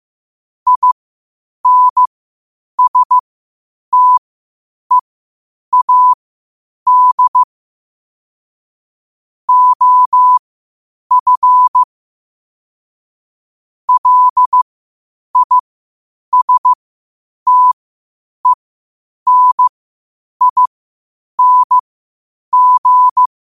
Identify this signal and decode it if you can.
Morse code, heard as a keyed tone, about 15 wpm, spaced as for 8 wpm, 1000 Hz. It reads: INSTEAD OF LISTENING
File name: qotdfiles / 8wpm